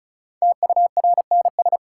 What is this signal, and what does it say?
Morse code, heard as a keyed tone, about 36 wpm, 700 Hz. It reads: TURNS
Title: Sets of 5 Words Encore - New Word Drill - 35wpm